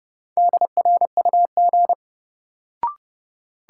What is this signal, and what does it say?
Morse code, heard as a keyed tone, about 30 wpm, 700 Hz. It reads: DRUG
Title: Sets of 5 Words - New Word Drill - 30wpm